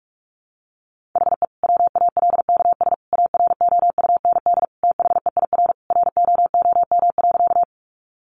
Morse code, heard as a keyed tone, about 44 wpm, 700 Hz. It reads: HE WALKS AROUND THEIR ROOM.
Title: Easy Phrases 45wpm